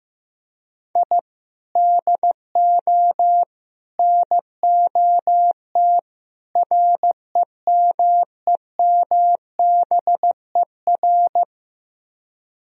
Morse code, heard as a keyed tone, about 15 wpm, 700 Hz. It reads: I DO NOT REMEMBER
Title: Speed-Racing - Sets of 4 Words 15wpm